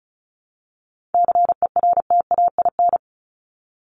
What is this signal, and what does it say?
Morse code, heard as a keyed tone, about 34 wpm, 700 Hz. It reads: CERTAIN
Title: LY Suffix - Top 100 Words with Base Words 35wpm